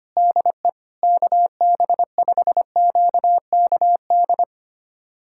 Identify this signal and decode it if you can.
Morse code, heard as a keyed tone, about 25 wpm, 700 Hz. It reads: DE KB5QKD
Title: USA Callsign DE Callsign - 6 Characters - 25wpm